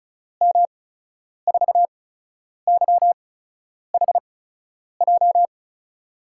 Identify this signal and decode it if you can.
Morse code, heard as a keyed tone, about 35 wpm, spaced as for 10 wpm, 700 Hz. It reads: M4YHJ
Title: USA Call Signs - 6 Characters - 8x Character Spacing - 35wpm